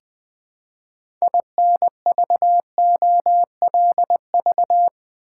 Morse code, heard as a keyed tone, about 21 wpm, 700 Hz. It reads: INVOLV
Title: Build A Word - 3 Letters At A Time - 8-Letter Words - 20wpm